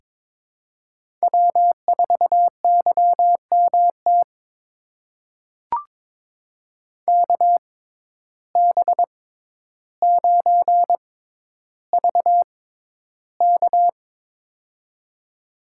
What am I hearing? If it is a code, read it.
Morse code, heard as a keyed tone, about 22 wpm, spaced as for 11 wpm, 700 Hz. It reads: W4YMT KB9VK
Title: USA Call Signs - 5 Characters - 6x Character Spacing - 22wpm